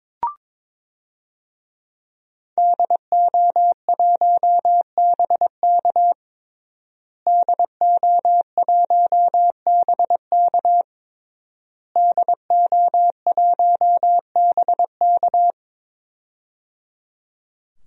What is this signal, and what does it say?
Morse code, heard as a keyed tone, about 22 wpm, 700 Hz. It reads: DO1BK DO1BK DO1BK
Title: DXCC-100 Call Signs - 5 Characters - Repeated 3x - 22wpm